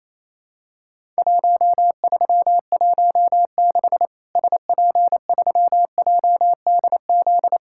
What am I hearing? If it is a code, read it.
Morse code, heard as a keyed tone, about 28 wpm, 700 Hz. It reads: 1316 SP3JDZ
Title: Worked All Europe DX Contest - QTC Exchange - 28wpm